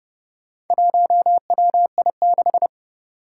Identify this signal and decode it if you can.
Morse code, heard as a keyed tone, about 28 wpm, 700 Hz. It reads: 1WI6
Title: Four Challenging Letter-Number Combinations 30wpm